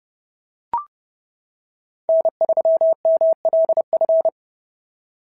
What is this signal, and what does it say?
Morse code, heard as a keyed tone, about 26 wpm, 650 Hz. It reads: N3MLF
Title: SOTA QSO Practice 30wpm